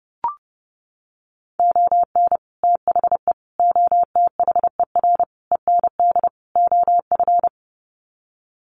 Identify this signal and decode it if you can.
Morse code, heard as a keyed tone, about 30 wpm, 700 Hz. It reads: ON THE OTHER END OF
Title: Sets of 5 Words Encore 30wpm